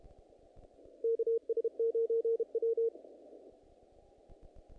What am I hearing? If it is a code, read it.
Morse code, heard as a keyed tone, about 32 wpm, 460 Hz. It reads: KS9W
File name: contests / naqp-cw-k5tr-op-k5pi- -2000